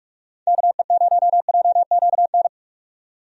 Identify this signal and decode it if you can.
Morse code, heard as a keyed tone, about 46 wpm, 700 Hz. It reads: KE0JQN
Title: USA Call Signs - 6 Characters - 45wpm